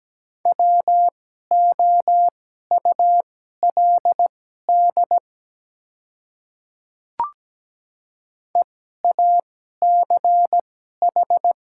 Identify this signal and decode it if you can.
Morse code, heard as a keyed tone, about 17 wpm, spaced as for 12 wpm, 700 Hz. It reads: WOULD EACH
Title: 2x Character Spacing - Top 100 Words - 17wpm